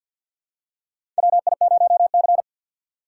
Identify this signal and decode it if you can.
Morse code, heard as a keyed tone, about 47 wpm, 700 Hz. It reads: WI0C
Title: USA Call Signs - 4 Characters - 2x Character Spacing - 50wpm